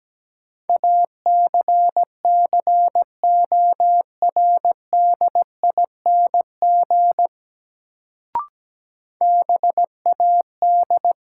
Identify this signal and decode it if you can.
Morse code, heard as a keyed tone, about 17 wpm, 700 Hz. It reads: ACCORDING BAD
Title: Sentences from Top 700 Words - New Word Drill - 17wpm